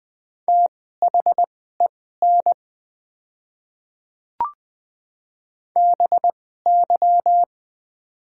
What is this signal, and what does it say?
Morse code, heard as a keyed tone, about 20 wpm, spaced as for 14 wpm, 700 Hz. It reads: THEN BY
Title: 2x Character Spacing - Top 100 Words - 20wpm